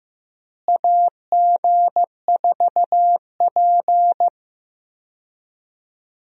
Morse code, heard as a keyed tone, about 15 wpm, 700 Hz. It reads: AG4P
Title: USA Call Signs - 4 Characters - Repeated 3x - 15wpm